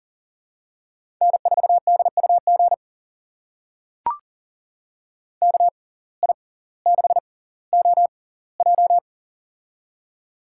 Morse code, heard as a keyed tone, about 40 wpm, spaced as for 20 wpm, 700 Hz. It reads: N4DUG KI6OJ
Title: USA Call Signs - 5 Characters - 6x Character Spacing - 40wpm